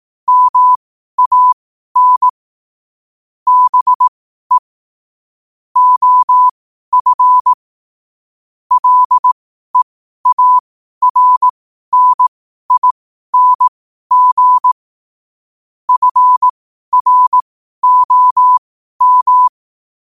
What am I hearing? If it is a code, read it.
Morse code, heard as a keyed tone, about 18 wpm, spaced as for 12 wpm, 1000 Hz. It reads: MAN BE OF LEARNING FROM
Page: New quotes every day in morse code at 12 Words per minute.